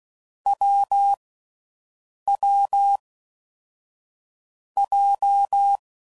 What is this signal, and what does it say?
Morse code, heard as a keyed tone, about 16 wpm, 800 Hz. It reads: W W J
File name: Combinations / Difficult Combos / AWJ1